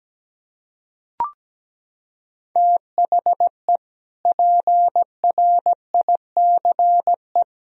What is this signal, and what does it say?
Morse code, heard as a keyed tone, about 17 wpm, 700 Hz. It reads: THE PRICE
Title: Sentences from Top 1000 Words - Review Part 4 - 17wpm